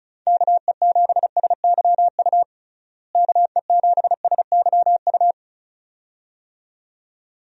Morse code, heard as a keyed tone, about 34 wpm, 700 Hz. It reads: KE7SYU KE7SYU
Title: USA Call Signs - 6 Characters - Repeated 2x - 35wpm